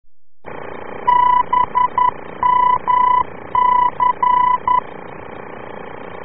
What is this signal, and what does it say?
Morse code, heard as a keyed tone, about 11 wpm, 990 Hz. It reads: BMC